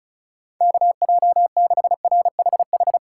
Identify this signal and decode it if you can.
Morse code, heard as a keyed tone, about 34 wpm, 700 Hz. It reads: KJ6RHH
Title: USA Call Signs - 6 Characters - 2x Character Spacing - 35wpm